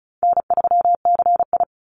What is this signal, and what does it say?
Morse code, heard as a keyed tone, about 34 wpm, 700 Hz. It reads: N3CI
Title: Call Signs Repeated 3x - 35wpm